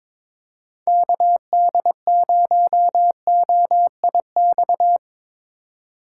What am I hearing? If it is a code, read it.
Morse code, heard as a keyed tone, about 22 wpm, 700 Hz. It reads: KD0OIX
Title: K1USN SST QSO Practice - 22wpm